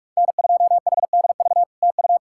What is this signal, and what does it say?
Morse code, heard as a keyed tone, about 48 wpm, 700 Hz. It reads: N2HDV TU
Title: SOTA QSO Practice 45wpm